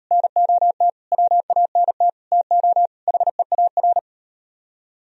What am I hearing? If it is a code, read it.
Morse code, heard as a keyed tone, about 37 wpm, 700 Hz. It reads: NOT WANT TO HEAR